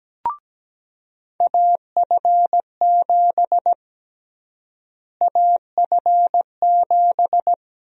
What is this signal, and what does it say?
Morse code, heard as a keyed tone, about 17 wpm, 700 Hz. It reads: AF7 AF7
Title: USA Call Sign Prefixes - Repeated 2x - 17wpm